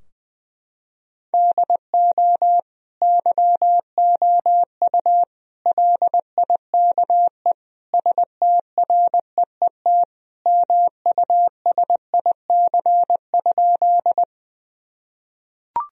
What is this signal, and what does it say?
Morse code, heard as a keyed tone, about 20 wpm, 700 Hz. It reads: DO YOU LIKE STREET MUSIC?
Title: Sentences from Top 1000 Words - Review Part 3 - 20wpm